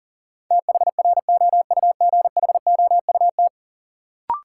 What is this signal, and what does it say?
Morse code, heard as a keyed tone, about 38 wpm, 700 Hz. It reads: THROUGHOUT
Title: Sentences from Top 800 Words - New Word Drill - 40wpm